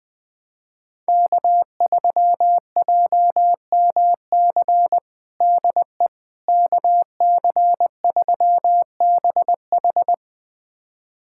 Morse code, heard as a keyed tone, about 20 wpm, 700 Hz. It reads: K3JMC DE KC3BH
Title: USA Callsign DE Callsign - 5 Characters - 20wpm